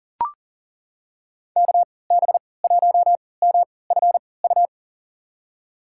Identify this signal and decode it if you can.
Morse code, heard as a keyed tone, about 40 wpm, spaced as for 23 wpm, 700 Hz. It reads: KB1MFU
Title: USA Call Signs - 3x Character Spacing - 40wpm